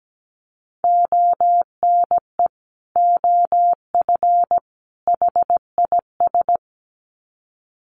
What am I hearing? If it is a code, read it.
Morse code, heard as a keyed tone, about 17 wpm, 700 Hz. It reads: ONE OF HIS
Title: Sets of 3 Words 17wpm